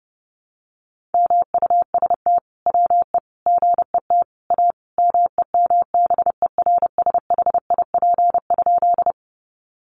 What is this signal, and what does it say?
Morse code, heard as a keyed tone, about 30 wpm, 700 Hz. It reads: MUST WE GET A MEMBERSHIP?
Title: Sentences from Top 1400 Words 30wpm